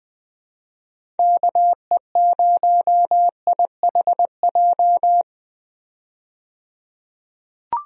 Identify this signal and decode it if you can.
Morse code, heard as a keyed tone, about 21 wpm, 700 Hz. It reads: KE0IHJ
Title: USA Call Signs - 6 Characters - 20wpm